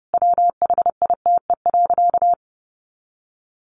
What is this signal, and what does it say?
Morse code, heard as a keyed tone, about 30 wpm, 700 Hz. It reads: WHITE.
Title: Sentences from Top 1500 Words 30wpm